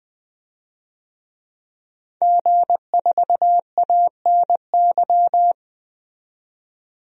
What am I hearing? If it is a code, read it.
Morse code, heard as a keyed tone, about 20 wpm, 700 Hz. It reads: G4ANY
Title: DXCC-100 Call Signs - 5 Characters - 20wpm